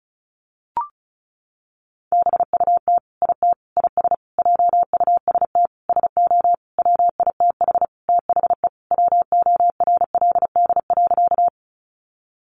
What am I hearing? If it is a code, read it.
Morse code, heard as a keyed tone, about 34 wpm, 700 Hz. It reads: BUT IT IS JUST SO WITH THE WORLD.
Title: Sentences from Top 200 Words 35wpm